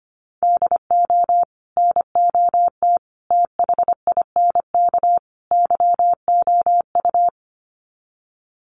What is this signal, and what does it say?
Morse code, heard as a keyed tone, about 25 wpm, 700 Hz. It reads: DO NOT THINK YOU
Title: Sets of 4 Words 25wpm